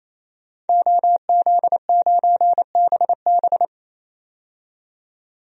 Morse code, heard as a keyed tone, about 28 wpm, 700 Hz. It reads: OZ9BB
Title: DXCC-100 Call Signs - 5 Characters - 28wpm